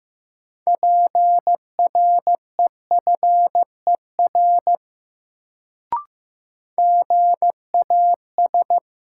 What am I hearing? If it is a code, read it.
Morse code, heard as a keyed tone, about 15 wpm, 700 Hz. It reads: PREFER GAS
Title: Sentences from Top 1100 Words - New Word Drill - 15wpm